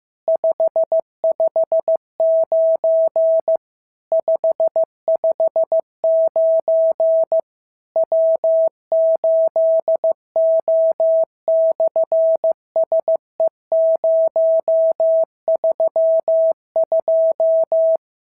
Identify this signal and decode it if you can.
Morse code, heard as a keyed tone, about 15 wpm, 650 Hz. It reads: 559 559 W8O/SE032
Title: SOTA QSO Practice 15wpm